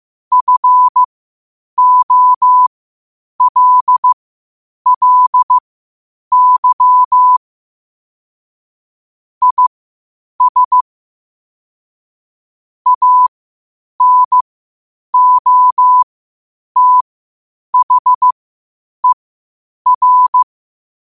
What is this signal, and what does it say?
Morse code, heard as a keyed tone, about 15 wpm, spaced as for 8 wpm, 1000 Hz. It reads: FOLLY IS ANOTHER
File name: qotdfiles / 8wpm